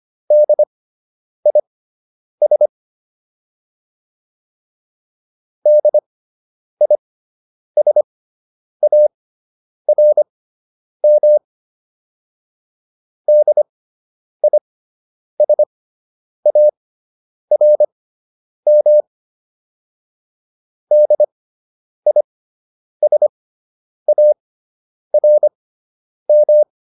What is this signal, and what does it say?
Morse code, heard as a keyed tone, about 25 wpm, spaced as for 9 wpm, 600 Hz. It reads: DIS DISARM DISARM DISARM